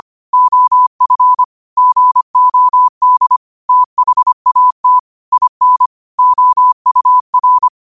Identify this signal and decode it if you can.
Morse code, heard as a keyed tone, about 25 wpm, 1000 Hz. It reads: OF GOD THAT IN OUR
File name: qotdfiles / 25wpm